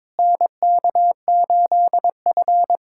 Morse code, heard as a keyed tone, about 22 wpm, 700 Hz. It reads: NK8F
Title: USA Callsign DE Callsign - 4 Characters - 22wpm